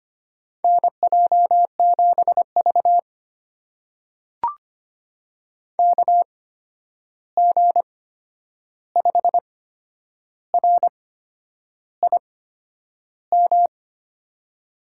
Morse code, heard as a keyed tone, about 25 wpm, spaced as for 9 wpm, 700 Hz. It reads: NJ7V KG5RIM
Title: USA Call Signs - 8x Character Spacing - 25wpm